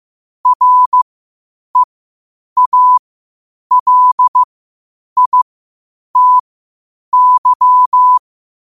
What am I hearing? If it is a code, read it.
Morse code, heard as a keyed tone, about 15 wpm, spaced as for 8 wpm, 1000 Hz. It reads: REALITY